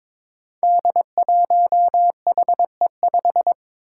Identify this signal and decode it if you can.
Morse code, heard as a keyed tone, about 21 wpm, 700 Hz. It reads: D1HE5
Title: Five Challenging Letter-Number Combinations 22wpm